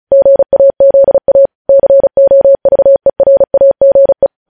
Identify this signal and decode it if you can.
Morse code, heard as a keyed tone, about 36 wpm, 550 Hz. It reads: GAZA COVERAGE
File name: podcast / News headlines-1x-35x35